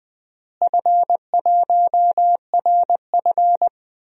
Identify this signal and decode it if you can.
Morse code, heard as a keyed tone, about 21 wpm, 700 Hz. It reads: F1RF
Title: DXCC-100 Call Signs - 4 Characters - Repeated 2x - 20wpm